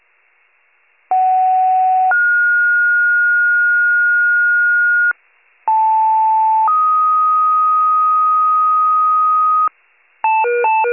Two-tone page, heard as a one-second tone, then a three-second tone, about 750 then 1430 Hz, about 880 then 1250 Hz, with noise. It is heard from 1.1 to 9.7 s.
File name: pager / 76 FIRE / wcalert